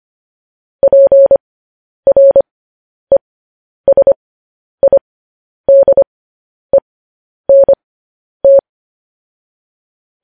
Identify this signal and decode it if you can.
Morse code, heard as a keyed tone, about 25 wpm, spaced as for 10 wpm, 550 Hz. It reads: PRESIDENT
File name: podcast / News headlines-1x-25x10